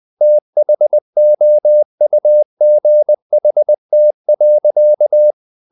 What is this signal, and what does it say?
Morse code, heard as a keyed tone, about 20 wpm, 600 Hz. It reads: THOUGHT.
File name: the old man and the sea part 2 20wpm